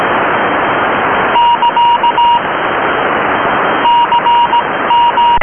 Morse code, heard as a keyed tone, about 17 wpm, spaced as for 15 wpm, 2850 Hz, with noise. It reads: <KA> CM